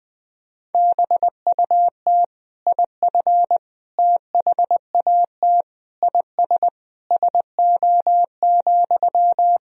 Morse code, heard as a keyed tone, about 20 wpm, 700 Hz. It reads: BUT IF THAT IS SO,
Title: Sentences from Top 200 Words and Abbreviations 20wpm